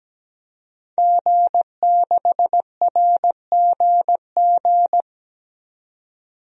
Morse code, heard as a keyed tone, about 17 wpm, 700 Hz. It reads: G6RGG
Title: DXCC-100 Call Signs - 5 Characters - 17wpm